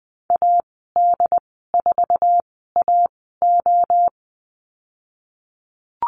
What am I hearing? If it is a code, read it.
Morse code, heard as a keyed tone, about 20 wpm, spaced as for 15 wpm, 700 Hz. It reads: AD4AO
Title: 122 Lesson 20g - Number 3 - Call Signs 20wpm